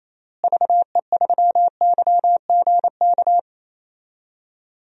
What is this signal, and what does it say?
Morse code, heard as a keyed tone, about 28 wpm, 700 Hz. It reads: VE3YGK
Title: DXCC-100 Call Signs - 6 Characters - Repeated 3x - 28wpm